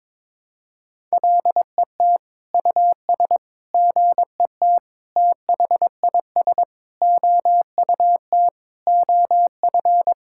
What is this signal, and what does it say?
Morse code, heard as a keyed tone, about 22 wpm, 700 Hz. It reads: LET US GET THIS OUT OF
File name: Sentences with 7-Word Phrases - 22wpm